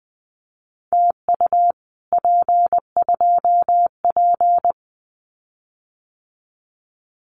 Morse code, heard as a keyed tone, about 20 wpm, 700 Hz. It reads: TU P2P
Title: Common QSO Phrases Encore 20wpm